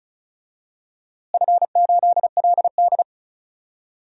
Morse code, heard as a keyed tone, about 36 wpm, 690 Hz. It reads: F8LD
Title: DXCC-100 Call Signs - 4 Characters - Repeated 2x - 35wpm